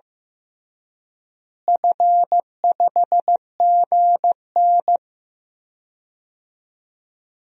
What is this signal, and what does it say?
Morse code, heard as a keyed tone, about 15 wpm, 700 Hz. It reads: F5GN
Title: DXCC-100 Call Signs - 4 Characters - Repeated 3x - 15wpm